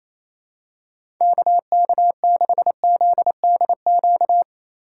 Morse code, heard as a keyed tone, about 28 wpm, 700 Hz. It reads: KK6ZDQ
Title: USA Call Signs - 6 Characters - 3x Character Spacing - 28wpm